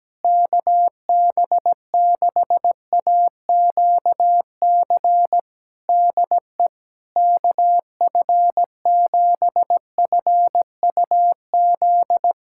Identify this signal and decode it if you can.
Morse code, heard as a keyed tone, about 17 wpm, 700 Hz. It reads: KB6AQC DE KF7FUZ
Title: Practice QSO - Exchange 4 - Ending - 17wpm